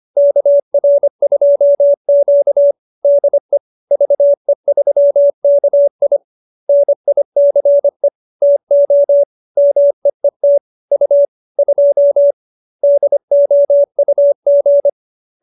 Morse code, heard as a keyed tone, about 25 wpm, 570 Hz. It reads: KR2Q DE VE3KI NICE TO MEET U 2 DOUG